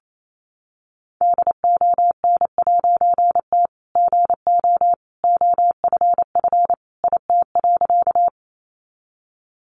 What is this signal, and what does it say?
Morse code, heard as a keyed tone, about 28 wpm, 700 Hz. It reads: DON'T GO OFF IT.
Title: A Sound Of Thunder 28wpm